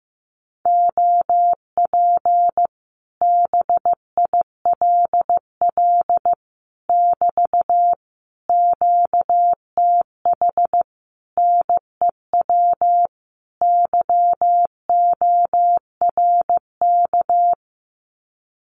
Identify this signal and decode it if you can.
Morse code, heard as a keyed tone, about 15 wpm, 700 Hz. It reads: OP BILL <BT> QTH NEW YORK
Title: Common QSO Phrases Encore 15wpm